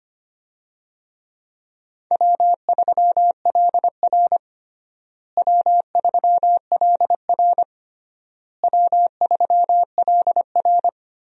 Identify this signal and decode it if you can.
Morse code, heard as a keyed tone, about 25 wpm, 700 Hz. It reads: W3LR W3LR W3LR
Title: USA Call Signs - 4 Characters - Repeated 3x - 25wpm